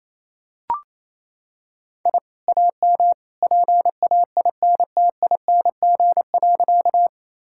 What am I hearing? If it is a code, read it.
Morse code, heard as a keyed tone, about 28 wpm, 700 Hz. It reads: I AM PAINTING.
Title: Sentences from Top 1000 Words - Review Part 6 - 28wpm